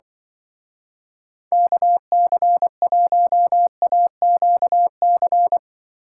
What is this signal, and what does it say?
Morse code, heard as a keyed tone, about 25 wpm, 700 Hz. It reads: KC1AQC
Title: Speed Racing - USA Call Signs 20wpm